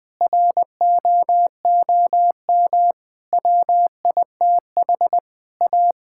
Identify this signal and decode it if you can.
Morse code, heard as a keyed tone, about 21 wpm, 700 Hz. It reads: ROOM WITH A
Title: Sentences from Top 1600 Words 20wpm